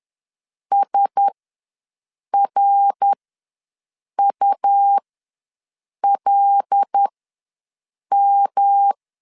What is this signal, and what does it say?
Morse code, heard as a keyed tone, about 10 wpm, spaced as for 6 wpm, 800 Hz. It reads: SRULM
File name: LEZIONI / 5 LBVC / 5 LBVC ws 10wpm 3733